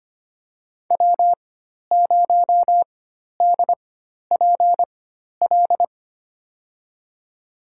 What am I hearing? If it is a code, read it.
Morse code, heard as a keyed tone, about 25 wpm, spaced as for 12 wpm, 700 Hz. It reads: W0DPL
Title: USA Call Signs - 5 Characters - 4x Character Spacing - 25wpm